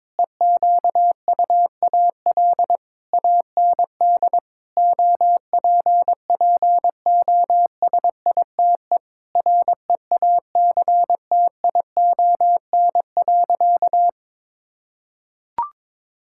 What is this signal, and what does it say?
Morse code, heard as a keyed tone, about 22 wpm, 700 Hz. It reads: EQUAL AND OPPOSITE REACTION.
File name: practice / 22wpm / Sentences with 8-Word Phrases - 22wpm